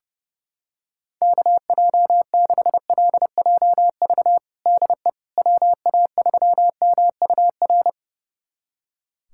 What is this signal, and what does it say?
Morse code, heard as a keyed tone, about 30 wpm, 700 Hz. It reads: KJ6LJV DE WA3MUR
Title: USA Callsign DE Callsign - 6 Characters - 30wpm